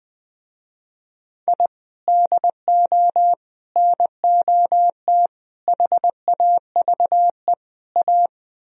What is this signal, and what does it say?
Morse code, heard as a keyed tone, about 21 wpm, 700 Hz. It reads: I DO NOT HAVE A
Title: Speed-Racing - Sets of 5 Words 20wpm